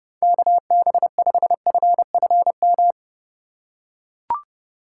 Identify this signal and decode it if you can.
Morse code, heard as a keyed tone, about 30 wpm, 700 Hz. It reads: KB5FFM
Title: USA Call Signs - 6 Characters - 2x Character Spacing - 30wpm